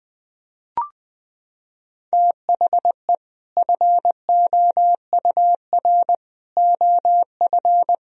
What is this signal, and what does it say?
Morse code, heard as a keyed tone, about 20 wpm, 700 Hz. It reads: THE FOUR OF
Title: Sentences from Top 1000 Words - Review Part 1 - 20wpm